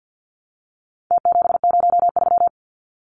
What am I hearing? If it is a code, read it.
Morse code, heard as a keyed tone, about 48 wpm, 700 Hz. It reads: T703
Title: Numbers - Contest Style - 4-Digit 50wpm